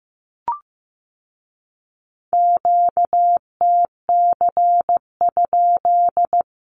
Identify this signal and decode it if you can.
Morse code, heard as a keyed tone, about 15 wpm, 700 Hz. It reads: QTC?
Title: Q-Codes with Spoken Meaning - 15wpm